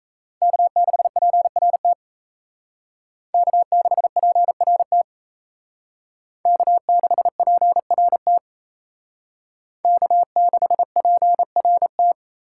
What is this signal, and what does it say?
Morse code, heard as a keyed tone, about 36 wpm, 700 Hz. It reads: K6PRT K6PRT K6PRT K6PRT